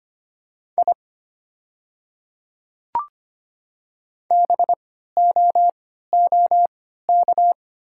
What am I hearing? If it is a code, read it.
Morse code, heard as a keyed tone, about 25 wpm, spaced as for 14 wpm, 700 Hz. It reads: I BOOK
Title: 3x Character Spacing - Top 300 Words - 25wpm